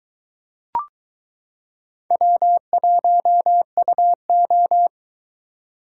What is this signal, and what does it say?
Morse code, heard as a keyed tone, about 23 wpm, 700 Hz. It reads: W1UO